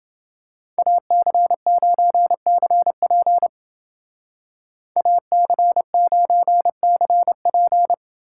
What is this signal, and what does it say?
Morse code, heard as a keyed tone, about 30 wpm, 700 Hz. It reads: AC9CP AC9CP